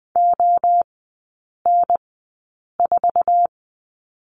Morse code, heard as a keyed tone, about 20 wpm, spaced as for 8 wpm, 700 Hz. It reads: ON4
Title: 053 Lesson 10e - Letter r - Three Characters 20wpm